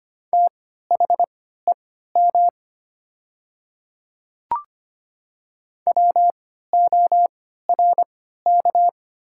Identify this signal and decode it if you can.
Morse code, heard as a keyed tone, about 25 wpm, spaced as for 14 wpm, 700 Hz. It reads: THEM WORK
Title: 3x Character Spacing - Top 200 Words - 25wpm